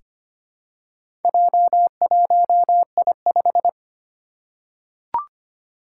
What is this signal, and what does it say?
Morse code, heard as a keyed tone, about 25 wpm, 700 Hz. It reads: J1I5
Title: Four Challenging Letter-Number Combinations 25wpm